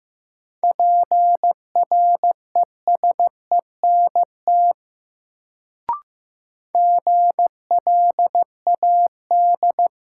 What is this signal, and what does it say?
Morse code, heard as a keyed tone, about 15 wpm, 700 Hz. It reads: PRESENT GLAD
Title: Sentences from Top 800 Words - New Word Drill - 15wpm